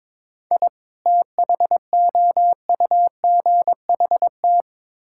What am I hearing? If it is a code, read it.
Morse code, heard as a keyed tone, about 21 wpm, 700 Hz. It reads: I THOUGHT
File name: Build A Phrase - 5-Word Phrases - 22wpm